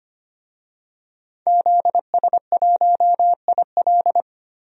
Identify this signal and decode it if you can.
Morse code, heard as a keyed tone, about 25 wpm, 700 Hz. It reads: ZS1IL
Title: DXCC-100 Call Signs - 5 Characters - 25wpm